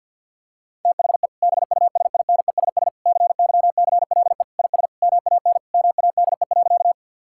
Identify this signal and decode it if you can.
Morse code, heard as a keyed tone, about 52 wpm, 700 Hz. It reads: THE BUSINESS CYCLE IS MAN MADE.
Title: Sentences from Top 1000 Words - Review Part 6 - 50wpm